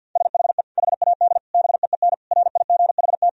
Morse code, heard as a keyed tone, about 50 wpm, 700 Hz. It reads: SHE HAD BEEN RIGHT
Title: Sentences from Top 1000 Words - Review Part 4 - 50wpm